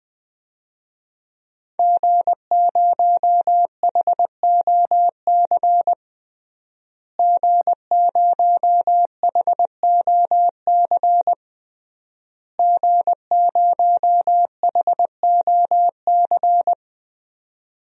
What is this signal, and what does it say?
Morse code, heard as a keyed tone, about 20 wpm, 700 Hz. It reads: G0HOC G0HOC G0HOC
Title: DXCC-100 Call Signs - 5 Characters - Repeated 3x - 20wpm